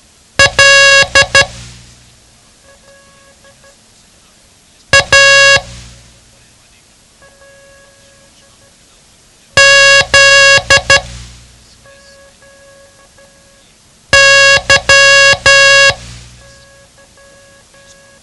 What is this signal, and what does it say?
Morse code, heard as a keyed tone, about 12 wpm, spaced as for 2 wpm, 1650 Hz. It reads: LAZY
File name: ham radio / CodeQuick / Code Quick 07